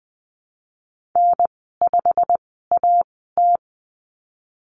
Morse code, heard as a keyed tone, about 20 wpm, spaced as for 15 wpm, 700 Hz. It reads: N5AT
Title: 140 Lesson 23g - Letter f - Call Signs 20wpm